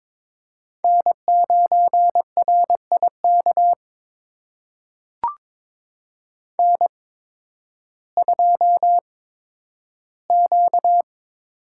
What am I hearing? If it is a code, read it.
Morse code, heard as a keyed tone, about 22 wpm, spaced as for 12 wpm, 700 Hz. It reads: N9RIK N2Q
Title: USA Call Signs - 5 Characters - 8x Character Spacing - 22wpm